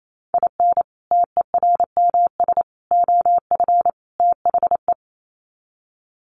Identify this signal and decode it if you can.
Morse code, heard as a keyed tone, about 28 wpm, 700 Hz. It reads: IN TERMS OF THE
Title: Sets of 4 Words 28wpm